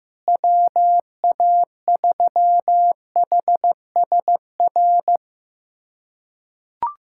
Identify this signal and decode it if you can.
Morse code, heard as a keyed tone, about 15 wpm, 700 Hz. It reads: WA3HSR
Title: USA Callsign DE Callsign 15wpm